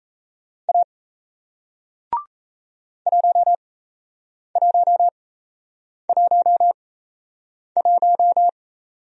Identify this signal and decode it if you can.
Morse code, heard as a keyed tone, about 38 wpm, spaced as for 8 wpm, 700 Hz. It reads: A 1111